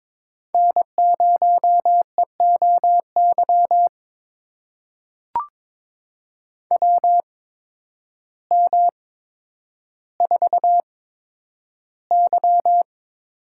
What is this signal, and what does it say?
Morse code, heard as a keyed tone, about 22 wpm, spaced as for 10 wpm, 700 Hz. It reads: N0EOY WM4Y
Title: USA Call Signs - 8x Character Spacing - 22wpm